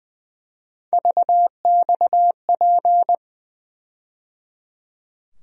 Morse code, heard as a keyed tone, about 20 wpm, 700 Hz. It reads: VXP
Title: USA Call Sign Postfixes - 3 Letters 20wpm